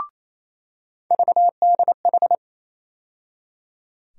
Morse code, heard as a keyed tone, about 28 wpm, 700 Hz. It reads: VDH
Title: USA Call Sign Postfixes - 3 Letters 28wpm